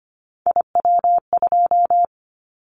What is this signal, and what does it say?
Morse code, heard as a keyed tone, about 25 wpm, 700 Hz. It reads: IW2
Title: Callsign Prefixes for Top 25 DXCC - Repeated 3x - 25wpm